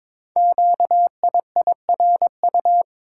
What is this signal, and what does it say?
Morse code, heard as a keyed tone, about 23 wpm, 700 Hz. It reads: QIIRU
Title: Code Groups - 5 Letters - 22wpm